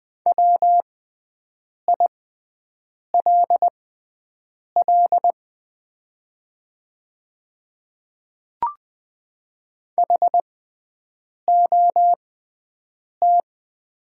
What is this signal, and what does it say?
Morse code, heard as a keyed tone, about 20 wpm, spaced as for 7 wpm, 700 Hz. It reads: WILL HOT
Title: 6x Character Spacing - Top 200 Words - 20wpm